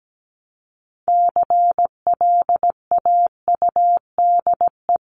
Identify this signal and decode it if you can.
Morse code, heard as a keyed tone, about 17 wpm, 700 Hz. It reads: CLAUDE
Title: CWT Exchange 17wpm